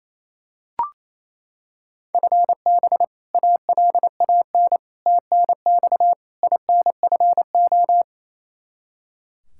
Morse code, heard as a keyed tone, about 28 wpm, 700 Hz. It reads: FB ALAN TNX INFO
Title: Practice QSO - Exchange 3 - More Info - 28wpm